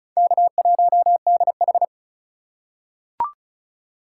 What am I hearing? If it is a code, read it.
Morse code, heard as a keyed tone, about 34 wpm, 700 Hz. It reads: K1DH
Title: USA Call Signs - 4 Characters - 2x Character Spacing - 35wpm